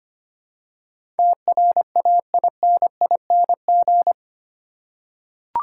Morse code, heard as a keyed tone, about 25 wpm, 700 Hz. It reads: TRAINING
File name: Eight Letter Words - 25wpm